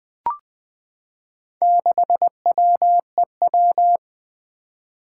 Morse code, heard as a keyed tone, about 21 wpm, 700 Hz. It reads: 6WEW
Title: Four Challenging Letter-Number Combinations 20wpm